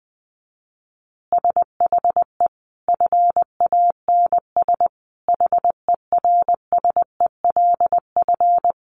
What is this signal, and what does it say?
Morse code, heard as a keyed tone, about 20 wpm, 700 Hz. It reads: SHE FANS HERSELF
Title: Easy Phrases 20wpm